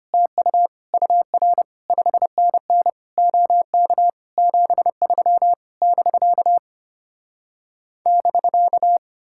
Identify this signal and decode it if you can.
Morse code, heard as a keyed tone, about 33 wpm, 700 Hz. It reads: TU UR 5NN OK 73 <BK> <BK>